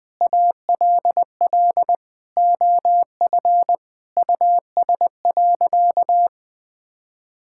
Morse code, heard as a keyed tone, about 20 wpm, 700 Hz. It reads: ALL OF US.
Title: Sentences with 3-Word Phrases - 20wpm